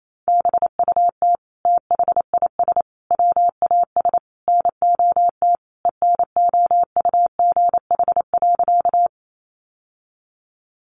Morse code, heard as a keyed tone, about 28 wpm, 700 Hz. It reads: BUT THIS WAS NOT ENOUGH.
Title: Sentences from Top 600 Words 28wpm